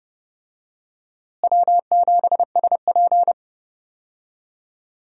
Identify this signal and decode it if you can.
Morse code, heard as a keyed tone, about 29 wpm, 700 Hz. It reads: W7SP
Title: USA Call Signs - 4 Characters - Repeated 3x - 30wpm